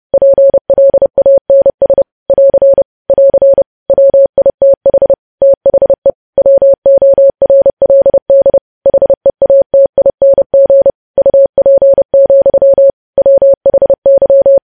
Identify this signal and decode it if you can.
Morse code, heard as a keyed tone, about 30 wpm, 550 Hz. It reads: PLANS <AR> <AR> WITH THE WORLD HEATING UP, WHY
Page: news headlines in morse code at 30 wpm